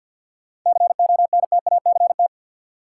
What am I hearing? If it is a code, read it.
Morse code, heard as a keyed tone, about 52 wpm, 700 Hz. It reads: CONTACT
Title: Sentences from Top 1000 Words - Review Part 7 - New Word Drill - 50wpm